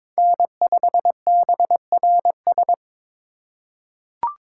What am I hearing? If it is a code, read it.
Morse code, heard as a keyed tone, about 22 wpm, 700 Hz. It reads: N5BRS
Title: USA Call Signs - 5 Characters - 2x Character Spacing - 22wpm